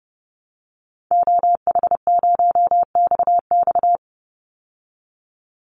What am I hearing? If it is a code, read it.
Morse code, heard as a keyed tone, about 30 wpm, 700 Hz. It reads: OH0XX
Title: Call Signs Repeated 3x - 30wpm